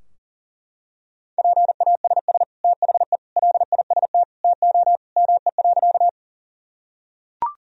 Morse code, heard as a keyed tone, about 37 wpm, 700 Hz. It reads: PASS THE LIST TO ME.
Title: Sentences from Top 1000 Words - Review Part 2 - 40wpm